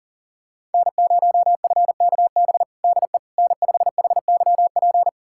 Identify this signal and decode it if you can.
Morse code, heard as a keyed tone, about 40 wpm, 700 Hz. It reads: N0FKB DE N5HYP
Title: USA Callsign DE Callsign - 5 Characters - 40wpm